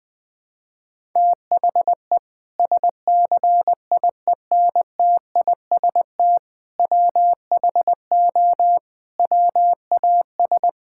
Morse code, heard as a keyed tone, about 20 wpm, 700 Hz. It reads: THE SCIENTIST WHO WAS
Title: Sentences with 8-Word Phrases - 20wpm